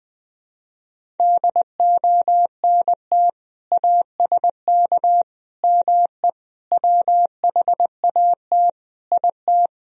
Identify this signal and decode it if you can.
Morse code, heard as a keyed tone, about 20 wpm, 700 Hz. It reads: DONT ASK ME WHAT IT
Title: Sentences from Top 1000 Words - Review Part 3 - 20wpm